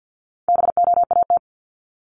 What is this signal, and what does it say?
Morse code, heard as a keyed tone, about 50 wpm, 700 Hz. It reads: BOAT
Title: Multi-letter Phonemes Repeated 3x 50wpm